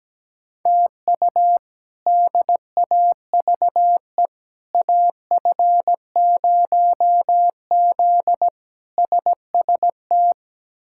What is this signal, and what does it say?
Morse code, heard as a keyed tone, about 17 wpm, 700 Hz. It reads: TU DAVE AF0Z SST
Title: K1USN SST QSO Practice - 17wpm